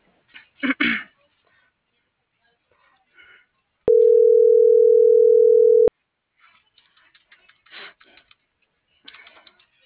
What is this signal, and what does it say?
Ringback tone (North America), 440 + 480 Hz.